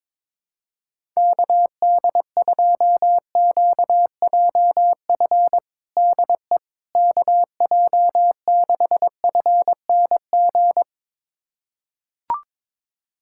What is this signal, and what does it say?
Morse code, heard as a keyed tone, about 22 wpm, 700 Hz. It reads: KD2QJF DE KJ6FNG
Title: USA Callsign DE Callsign - 6 Characters - 22wpm